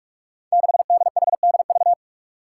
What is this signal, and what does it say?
Morse code, heard as a keyed tone, about 44 wpm, 700 Hz. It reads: 6DHDV
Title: Five Challenging Letter-Number Combinations 45wpm